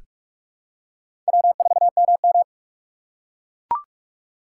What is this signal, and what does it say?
Morse code, heard as a keyed tone, about 43 wpm, 700 Hz. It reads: W4MM
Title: USA Call Signs - 4 Characters - 4x Character Spacing - 45wpm